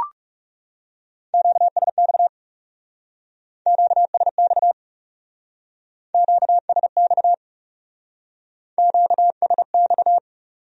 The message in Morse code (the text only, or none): QSX QSX QSX QSX